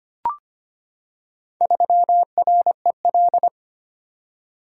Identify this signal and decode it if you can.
Morse code, heard as a keyed tone, about 25 wpm, 690 Hz. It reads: 3REL
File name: practice / 25wpm / Four Letter-Number Combinations - 25wpm